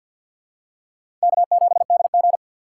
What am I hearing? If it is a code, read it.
Morse code, heard as a keyed tone, about 52 wpm, 700 Hz. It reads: K7DG
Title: USA Call Signs - 4 Characters - 3x Character Spacing - 50wpm